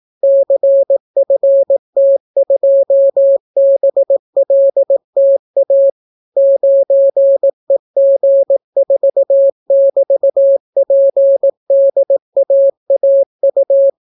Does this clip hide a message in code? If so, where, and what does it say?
Yes: Morse code from 0.2 to 13.9 s, heard as a keyed tone, about 18 wpm, 550 Hz. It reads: CFT2BLTA 9EG4<BT>PDAAU